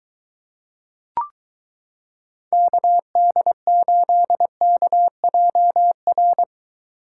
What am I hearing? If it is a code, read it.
Morse code, heard as a keyed tone, about 22 wpm, 700 Hz. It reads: KD8KJR